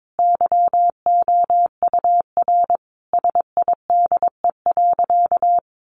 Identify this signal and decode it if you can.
Morse code, heard as a keyed tone, about 22 wpm, 700 Hz. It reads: YOUR SIDE.
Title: Sentences from Top 400 Words 22wpm